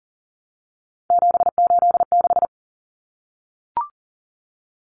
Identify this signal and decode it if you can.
Morse code, heard as a keyed tone, about 40 wpm, 700 Hz. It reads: 786
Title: Common Cut Numbers - 3-Digit 40wpm